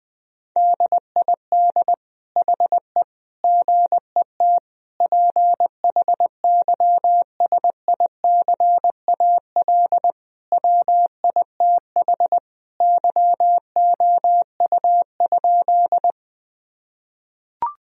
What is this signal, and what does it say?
Morse code, heard as a keyed tone, about 21 wpm, 700 Hz. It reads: DID HE GET PHYSICAL WITH YOU?
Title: Sentences from Top 1000 Words - Review Part 4 - 20wpm